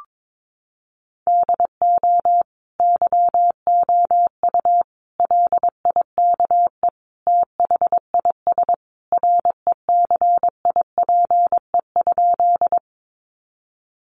Morse code, heard as a keyed tone, about 22 wpm, 700 Hz. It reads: DO YOU LIKE THIS RECIPE?
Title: Sentences from Top 1200 Words 22wpm